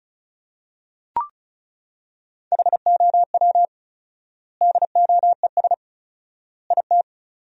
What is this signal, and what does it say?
Morse code, heard as a keyed tone, about 36 wpm, 700 Hz. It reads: HOW DOES IT
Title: 4x Word-Spacing - Sentences from Top 400 Words - 35wpm